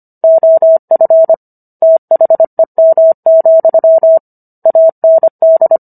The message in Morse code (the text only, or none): OF THEM, AND